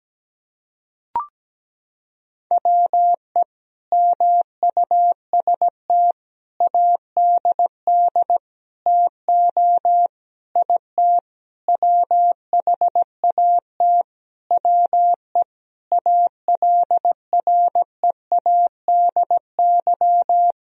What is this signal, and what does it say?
Morse code, heard as a keyed tone, about 17 wpm, 700 Hz. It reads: WE MUST ADD TO IT WHAT WE ALREADY
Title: Sentences from Top 1000 Words - Review Part 1 - 17wpm